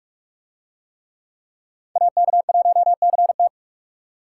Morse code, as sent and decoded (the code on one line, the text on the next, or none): .- -.- .---- -.-. -
AK1CT